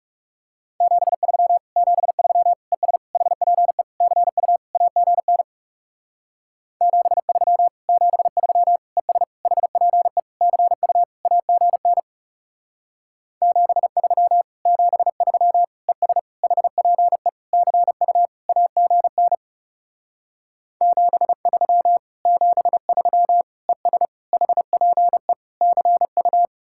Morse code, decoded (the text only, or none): 73 73 ES HPE CU AGN 73 73 ES HPE CU AGN 73 73 ES HPE CU AGN 73 73 ES HPE CU